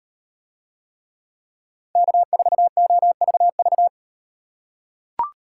Morse code, heard as a keyed tone, about 38 wpm, 700 Hz. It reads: K4OVV